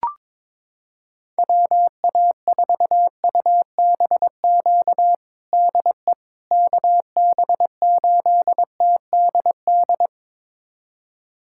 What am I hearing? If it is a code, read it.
Morse code, heard as a keyed tone, about 22 wpm, 700 Hz. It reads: WA4UBQ DE KB8TDD
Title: USA Callsign DE Callsign - 6 Characters - 22wpm